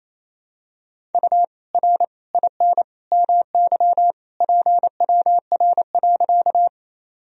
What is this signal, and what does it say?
Morse code, heard as a keyed tone, about 28 wpm, 700 Hz. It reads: U R IN MY PWR.
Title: Sentences from Top 500 Words and Abbreviations 28wpm